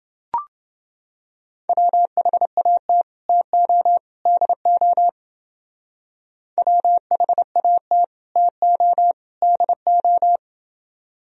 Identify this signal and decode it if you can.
Morse code, heard as a keyed tone, about 29 wpm, 700 Hz. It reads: WHAT TO DO WHAT TO DO